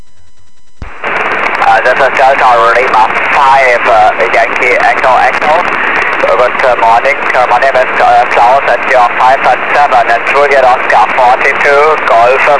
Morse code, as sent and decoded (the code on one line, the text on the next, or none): none